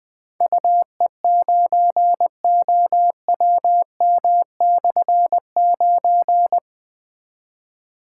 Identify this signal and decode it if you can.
Morse code, heard as a keyed tone, about 20 wpm, 700 Hz. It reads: UE9OWM/9
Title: Difficult Call Signs 20wpm